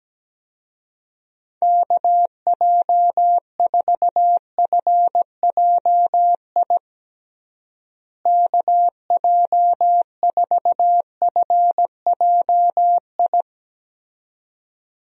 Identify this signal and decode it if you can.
Morse code, heard as a keyed tone, about 17 wpm, 700 Hz. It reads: KJ4FJI KJ4FJI
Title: USA Call Signs - 6 Characters - Repeated 2x - 17wpm